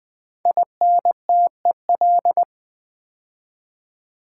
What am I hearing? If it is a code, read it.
Morse code, heard as a keyed tone, about 20 wpm, 700 Hz. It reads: INTEL
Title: Build A Word - 5 Letters At A Time - 12-Letter Words - 20wpm